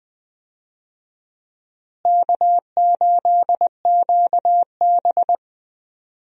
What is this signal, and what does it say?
Morse code, heard as a keyed tone, about 20 wpm, 700 Hz. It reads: K8QB
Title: USA Call Signs - 4 Characters - Repeated 3x - 20wpm